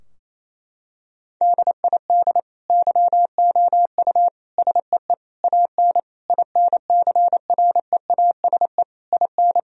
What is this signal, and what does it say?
Morse code, heard as a keyed tone, about 28 wpm, 700 Hz. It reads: DID YOU SEE AN INCREASE IN
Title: Sentences from Top 1000 Words - Review Part 4 - 28wpm